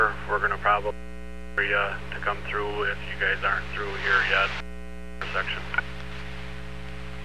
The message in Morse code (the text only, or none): none